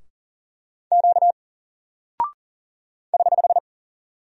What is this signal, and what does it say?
Morse code, heard as a keyed tone, about 40 wpm, 700 Hz. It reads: Q <HH>
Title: Single Character (Letter:Number:Punctuation:Prosign) Alt 40wpm